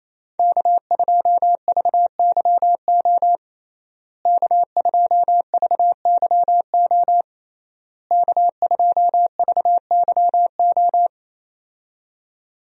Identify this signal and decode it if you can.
Morse code, heard as a keyed tone, about 27 wpm, 700 Hz. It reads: K2VYO K2VYO K2VYO
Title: USA Call Signs - 5 Characters - Repeated 3x - 28wpm